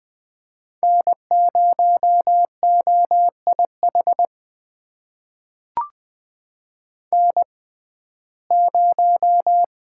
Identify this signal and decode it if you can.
Morse code, heard as a keyed tone, about 20 wpm, spaced as for 15 wpm, 700 Hz. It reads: N0OIH N0
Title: USA Call Signs - 5 Characters - 6x Character Spacing - 20wpm